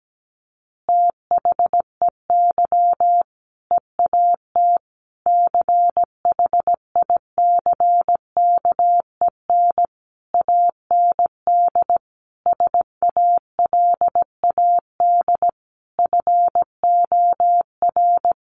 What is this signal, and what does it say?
Morse code, heard as a keyed tone, about 17 wpm, 700 Hz. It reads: THEY EAT CHICKEN AND SALAD FOR
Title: Easy Phrases 17wpm